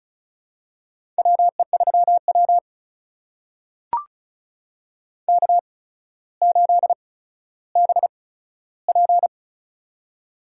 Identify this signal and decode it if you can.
Morse code, heard as a keyed tone, about 35 wpm, spaced as for 15 wpm, 700 Hz. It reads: WE3W K8BP
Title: USA Call Signs - 4 Characters - 8x Character Spacing - 35wpm